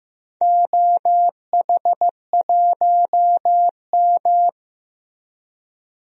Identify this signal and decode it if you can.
Morse code, heard as a keyed tone, about 15 wpm, 700 Hz. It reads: OH1M
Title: DXCC-100 Call Signs - 4 Characters - Repeated 2x - 15wpm